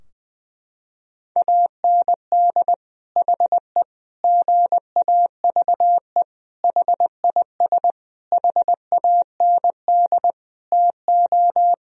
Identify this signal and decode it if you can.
Morse code, heard as a keyed tone, about 20 wpm, 700 Hz. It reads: AND HE GAVE HIS HAND TO
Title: Sentences from Top 1000 Words - Review Part 4 - 20wpm